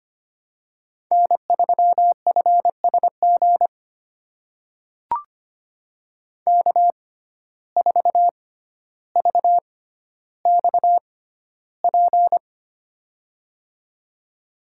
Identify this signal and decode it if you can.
Morse code, heard as a keyed tone, about 25 wpm, spaced as for 13 wpm, 700 Hz. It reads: N3FSG K4VXP
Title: USA Call Signs - 5 Characters - 6x Character Spacing - 25wpm